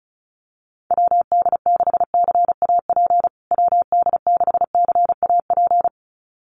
Morse code, heard as a keyed tone, about 34 wpm, 700 Hz. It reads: WD6CAP WD6CAP
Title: USA Call Signs Repeated 2x 35wpm